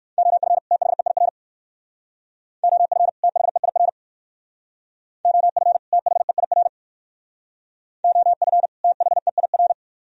Morse code, heard as a keyed tone, about 58 wpm, 700 Hz. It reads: OF THEIR OF THEIR OF THEIR OF THEIR